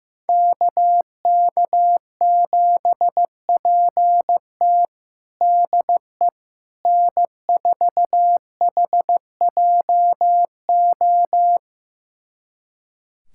Morse code, heard as a keyed tone, about 15 wpm, 700 Hz. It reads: KK7PT DE N4HJO
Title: USA Callsign DE Callsign - 5 Characters - 15wpm